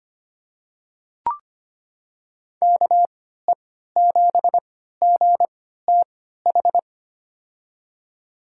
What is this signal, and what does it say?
Morse code, heard as a keyed tone, about 25 wpm, spaced as for 14 wpm, 700 Hz. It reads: KE7GTH
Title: USA Call Signs - 6 Characters - 3x Character Spacing - 25wpm